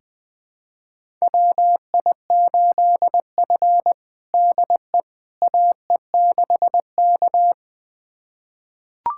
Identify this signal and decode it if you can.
Morse code, heard as a keyed tone, about 21 wpm, 700 Hz. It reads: WI8F DE AE6K
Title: USA Callsign DE Callsign - 4 Characters - 20wpm